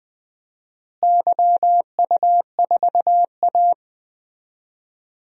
Morse code, heard as a keyed tone, about 20 wpm, 700 Hz. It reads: YU4A
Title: DXCC-100 Call Signs - 4 Characters - Repeated 3x - 20wpm